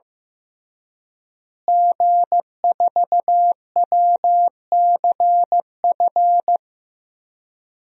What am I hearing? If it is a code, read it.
Morse code, heard as a keyed tone, about 16 wpm, 700 Hz. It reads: G4WCF
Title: DXCC-100 Call Signs - 5 Characters - Repeated 3x - 15wpm